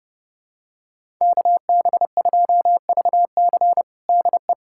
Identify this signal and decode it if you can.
Morse code, heard as a keyed tone, about 30 wpm, 700 Hz. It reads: KB2VC DE
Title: USA Callsign DE Callsign - 5 Characters - 30wpm